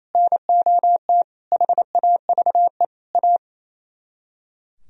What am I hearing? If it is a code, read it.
Morse code, heard as a keyed tone, about 36 wpm, 700 Hz. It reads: NOT HAVE A